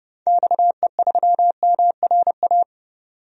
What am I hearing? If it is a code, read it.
Morse code, heard as a keyed tone, about 32 wpm, 700 Hz. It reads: XE3MRA
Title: DXCC-100 Call Signs - 6 Characters - 30wpm